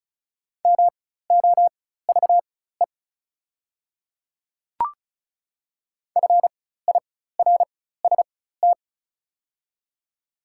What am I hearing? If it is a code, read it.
Morse code, heard as a keyed tone, about 36 wpm, spaced as for 16 wpm, 700 Hz. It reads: MOVE FIRST
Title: 4x Character Spacing - Top 200 Words - 35wpm